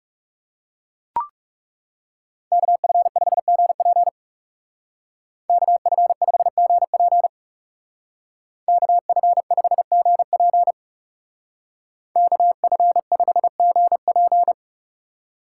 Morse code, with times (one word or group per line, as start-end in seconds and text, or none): 2.5-4.1 KF5GP
5.5-7.3 KF5GP
8.7-10.7 KF5GP
12.2-14.5 KF5GP